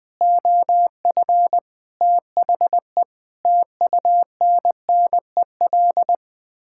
Morse code, heard as a keyed tone, about 21 wpm, 700 Hz. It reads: OF THE TUNNEL
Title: Sets of 8 Words - 20wpm